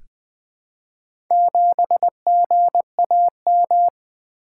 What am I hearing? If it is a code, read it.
Morse code, heard as a keyed tone, about 20 wpm, 700 Hz. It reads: 7GAM
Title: Four Letter-Number Combinations - 20wpm